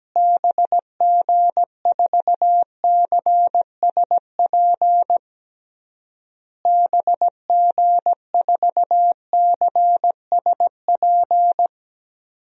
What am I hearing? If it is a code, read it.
Morse code, heard as a keyed tone, about 17 wpm, 700 Hz. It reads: BG4CSP BG4CSP
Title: DXCC-100 Call Signs - 6 Characters - Repeated 3x - 17wpm